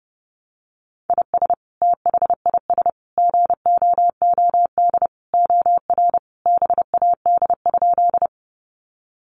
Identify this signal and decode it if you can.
Morse code, heard as a keyed tone, about 28 wpm, 700 Hz. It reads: IS THIS GOOD OR BAD?
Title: Sentences from Top 700 Words 30wpm